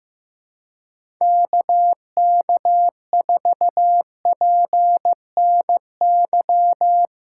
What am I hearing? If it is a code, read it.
Morse code, heard as a keyed tone, about 15 wpm, 700 Hz. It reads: KK4PNY
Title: USA Call Signs - 2x Character Spacing - 15wpm